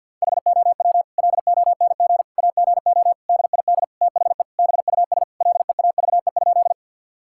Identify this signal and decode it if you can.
Morse code, heard as a keyed tone, about 50 wpm, 700 Hz. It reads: HOW LONG AGO DID THE BUS LEAVE?
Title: Sentences from Top 1000 Words - Review Part 6 - 50wpm